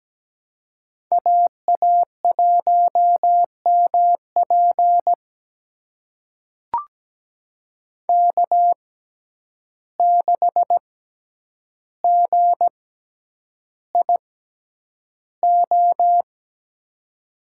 Morse code, heard as a keyed tone, about 17 wpm, spaced as for 9 wpm, 700 Hz. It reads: AA1MP K6GIO
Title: USA Call Signs - 5 Characters - 6x Character Spacing - 17wpm